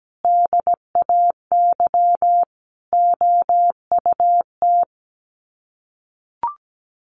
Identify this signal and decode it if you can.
Morse code, heard as a keyed tone, about 17 wpm, 700 Hz. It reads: DAY OUT
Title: Sets of 5 Words Encore2 17wpm